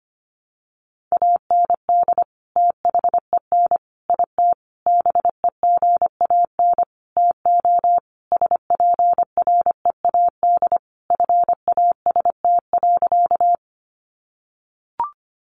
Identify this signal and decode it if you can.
Morse code, heard as a keyed tone, about 25 wpm, 700 Hz. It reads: AND THEN IT BEGAN TO SPREAD FAST.
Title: Sentences from Top 1200 Words 25wpm